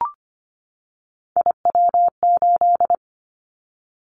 Morse code, heard as a keyed tone, about 25 wpm, 700 Hz. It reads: IW8
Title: Callsign Prefixes for Top 25 DXCC - 25wpm